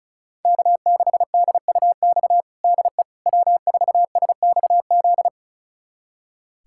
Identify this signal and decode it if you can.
Morse code, heard as a keyed tone, about 36 wpm, 700 Hz. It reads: K6DUX DE W4SXZ
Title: USA Callsign DE Callsign - 5 Characters - 35wpm